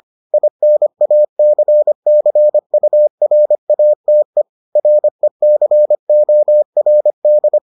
Morse code, heard as a keyed tone, about 25 wpm, 600 Hz. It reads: INACCURATE RECORD